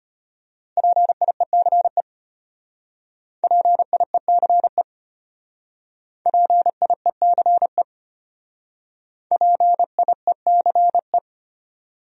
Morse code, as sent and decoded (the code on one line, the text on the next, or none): .--. .. -.-. / .--. .. -.-. / .--. .. -.-. / .--. .. -.-. .
PIECE PIECE PIECE PIECE